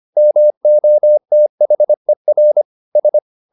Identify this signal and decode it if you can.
Morse code, heard as a keyed tone, about 25 wpm, 600 Hz. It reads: MOTHER S
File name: the old man and the sea part 1 25wpm